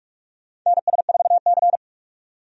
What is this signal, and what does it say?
Morse code, heard as a keyed tone, about 46 wpm, 700 Hz. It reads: NS4C
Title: USA Call Signs - 4 Characters - 45wpm